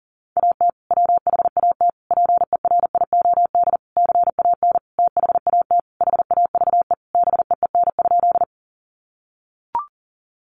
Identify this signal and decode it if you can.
Morse code, heard as a keyed tone, about 40 wpm, 700 Hz. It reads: AT WHAT PERIOD CAN THAT HAVE BEEN?
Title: Sentences from Top 800 Words 40wpm